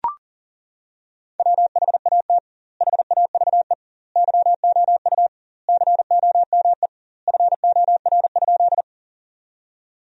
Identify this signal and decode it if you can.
Morse code, heard as a keyed tone, about 40 wpm, 700 Hz. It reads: WHAT HAVE YOU COME FOR?
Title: 2x Word-Spacing - Sentences from Top 100 Words - 40wpm